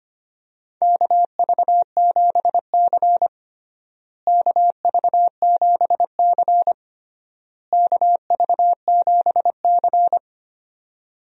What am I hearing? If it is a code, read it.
Morse code, heard as a keyed tone, about 25 wpm, 700 Hz. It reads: KV7C KV7C KV7C
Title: USA Call Signs - 4 Characters - Repeated 3x - 25wpm